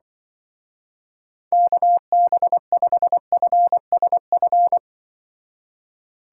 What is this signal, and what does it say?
Morse code, heard as a keyed tone, about 24 wpm, 700 Hz. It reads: KB5FSF
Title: Speed Racing - USA 2x3 Call Signs 20wpm